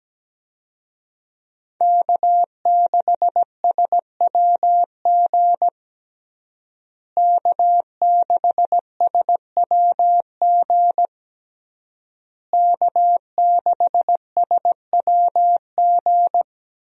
Morse code, heard as a keyed tone, about 17 wpm, 700 Hz. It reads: K6SWG K6SWG K6SWG
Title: USA Call Signs - 5 Characters - Repeated 3x - 17wpm